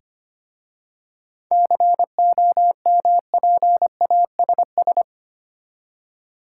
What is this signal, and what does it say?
Morse code, heard as a keyed tone, about 25 wpm, 690 Hz. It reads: COMPASS
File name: practice / 25wpm / Easy Words with Hints - 25wpm